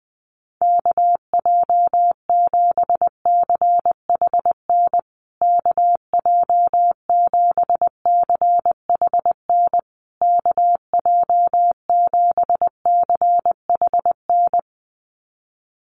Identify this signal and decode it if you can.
Morse code, heard as a keyed tone, about 20 wpm, 700 Hz. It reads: KJ7CHN KJ7CHN KJ7CHN
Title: USA Call Signs Repeated 3x 20wpm